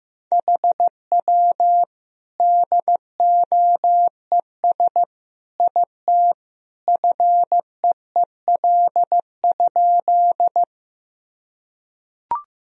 Morse code, heard as a keyed tone, about 15 wpm, 700 Hz. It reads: HW DOES IT FEEL?
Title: Sentences from Top 400 Words and Abbreviations 15wpm